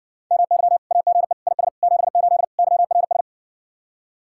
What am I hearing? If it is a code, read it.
Morse code, heard as a keyed tone, about 60 wpm, 700 Hz. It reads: MY AGE IS 78 YRS